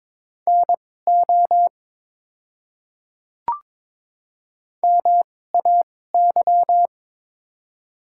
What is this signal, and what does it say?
Morse code, heard as a keyed tone, about 22 wpm, spaced as for 16 wpm, 700 Hz. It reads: NO MAY
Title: 2x Character Spacing - Top 100 Words - 22wpm